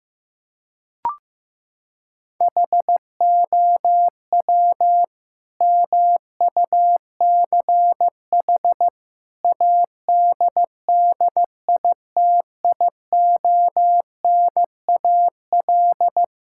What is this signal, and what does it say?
Morse code, heard as a keyed tone, about 15 wpm, 700 Hz. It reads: HOW MUCH ADDITIONAL
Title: Sentences from Top 1000 Words - Review Part 5 - 15wpm